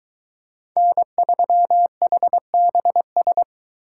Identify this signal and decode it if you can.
Morse code, heard as a keyed tone, about 22 wpm, 700 Hz. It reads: N3HBS